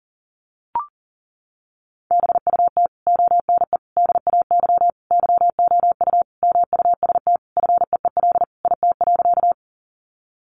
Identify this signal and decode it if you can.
Morse code, heard as a keyed tone, about 40 wpm, 700 Hz. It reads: BUT ONE DAY YOU MUST FEEL IT.
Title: Sentences from Top 400 Words 40wpm